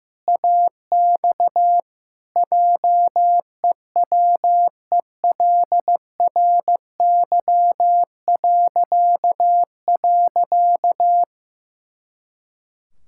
Morse code, heard as a keyed tone, about 15 wpm, 700 Hz. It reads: AX JEWELRY..
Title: Pangram Challenge v2 - 15wpm